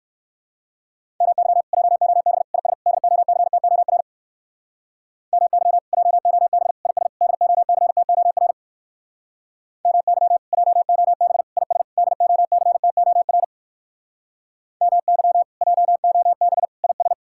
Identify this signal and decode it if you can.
Morse code, heard as a keyed tone, about 58 wpm, 700 Hz. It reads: MY JOB IS DOCTOR MY JOB IS DOCTOR MY JOB IS DOCTOR MY JOB IS